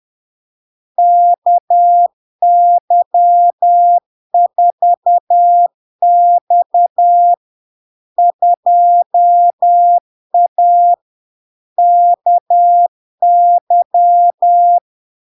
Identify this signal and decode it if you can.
Morse code, heard as a keyed tone, about 10 wpm, 700 Hz. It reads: KY4X 2A KY
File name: LARK-FD-2022-Log-in-CW-10wpm-700-Hz